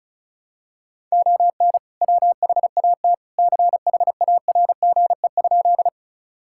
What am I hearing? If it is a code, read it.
Morse code, heard as a keyed tone, about 36 wpm, 700 Hz. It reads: ON WHAT CHARGE?
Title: Sentences from Top 1000 Words - Review Part 6 - 35wpm